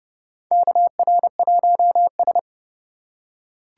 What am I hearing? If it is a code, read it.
Morse code, heard as a keyed tone, about 29 wpm, 690 Hz. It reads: KR1S
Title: USA Call Signs - 4x Character Spacing - 30wpm